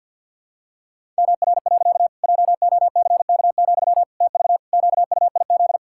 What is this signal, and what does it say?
Morse code, heard as a keyed tone, about 47 wpm, 700 Hz. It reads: MR. JOCK, TV QUIZ
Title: Pangram Challenge v2 - 50wpm